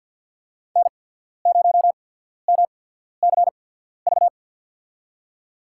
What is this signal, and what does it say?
Morse code, heard as a keyed tone, about 53 wpm, spaced as for 14 wpm, 700 Hz. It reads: N0MCV
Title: USA Call Signs - 5 Characters - 8x Character Spacing - 50wpm